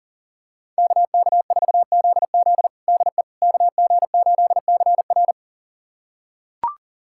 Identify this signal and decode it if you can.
Morse code, heard as a keyed tone, about 37 wpm, 700 Hz. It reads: KK4ZZ DE KG8CR
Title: USA Callsign DE Callsign - 5 Characters - 40wpm